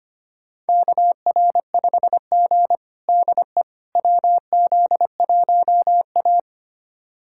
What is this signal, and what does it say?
Morse code, heard as a keyed tone, about 25 wpm, 700 Hz. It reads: KR5G DE WZ1A
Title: USA Callsign DE Callsign - 4 Characters - 25wpm